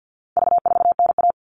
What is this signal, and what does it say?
Morse code, heard as a keyed tone, about 48 wpm, 700 Hz. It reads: V4NA
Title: All Cut Numbers - 4-Digit 50wpm